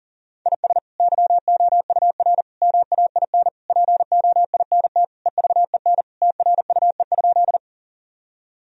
Morse code, heard as a keyed tone, about 37 wpm, 700 Hz. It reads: IS YOUR MAIN POINT EVEN TRUE?
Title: Sentences from Top 1000 Words - Review Part 2 - 40wpm